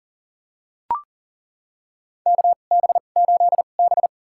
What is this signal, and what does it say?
Morse code, heard as a keyed tone, about 40 wpm, spaced as for 29 wpm, 690 Hz. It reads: KB8B
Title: USA Call Signs - 4 Characters - 2x Character Spacing - 40wpm